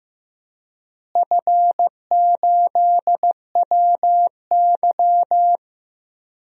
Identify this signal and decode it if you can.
Morse code, heard as a keyed tone, about 16 wpm, 700 Hz. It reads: F8WY
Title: DXCC-100 Call Signs - 4 Characters - Repeated 3x - 15wpm